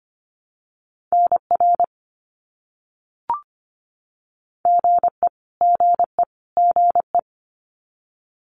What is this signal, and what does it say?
Morse code, heard as a keyed tone, about 25 wpm, 700 Hz. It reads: NR GE GE GE
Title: Abbreviations Repeated 3x - 25wpm